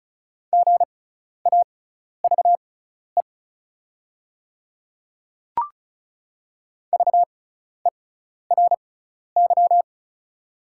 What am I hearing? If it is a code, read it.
Morse code, heard as a keyed tone, about 34 wpm, spaced as for 12 wpm, 700 Hz. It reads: GAVE VERY
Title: 6x Character Spacing - Top 500 Words - 35wpm